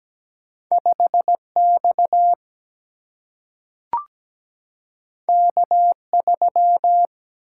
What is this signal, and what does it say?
Morse code, heard as a keyed tone, about 17 wpm, 700 Hz. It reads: 5X K3
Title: Single-Letter Single-Number 17wpm